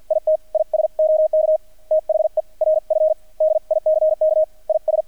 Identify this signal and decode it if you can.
Morse code, heard as a keyed tone, about 48 wpm, 630 Hz. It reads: IT IS OK THE RF DICK IS
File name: archived / CRSnet